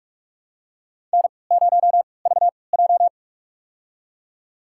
Morse code, heard as a keyed tone, about 43 wpm, spaced as for 25 wpm, 700 Hz. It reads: N0VJ
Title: USA Call Signs - 3x Character Spacing - 45wpm